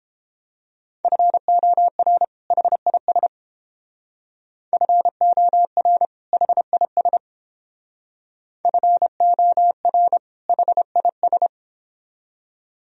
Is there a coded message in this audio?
Yes: Morse code from 1.0 to 11.5 s, heard as a keyed tone, about 30 wpm, 700 Hz. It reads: FOR HIS FOR HIS FOR HIS